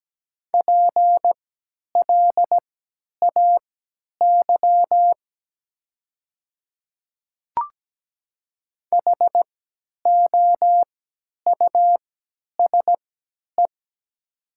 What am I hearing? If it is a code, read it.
Morse code, heard as a keyed tone, about 17 wpm, spaced as for 10 wpm, 700 Hz. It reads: PLAY HOUSE
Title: 3x Character Spacing - Top 200 Words - 17wpm